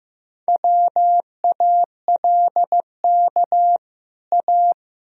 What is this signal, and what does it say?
Morse code, heard as a keyed tone, about 14 wpm, 700 Hz. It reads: WALK A
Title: Sentences from Top 1000 Words - Review Part 1 - 15wpm